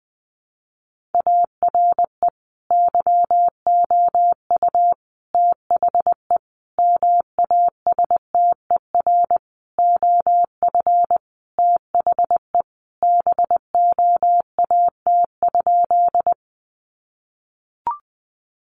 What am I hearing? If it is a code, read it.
Morse code, heard as a keyed tone, about 20 wpm, 700 Hz. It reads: ARE YOU THE MASTER OF THE BOAT?
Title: Sentences from Top 1300 Words 20wpm